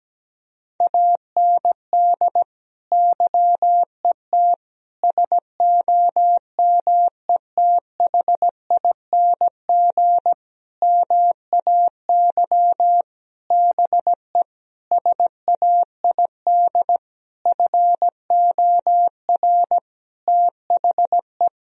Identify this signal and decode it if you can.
Morse code, heard as a keyed tone, about 17 wpm, 700 Hz. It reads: AND YET SOMETHING MAY BE SAID FOR THE
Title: Sentences from Top 1000 Words - Review Part 3 - 17wpm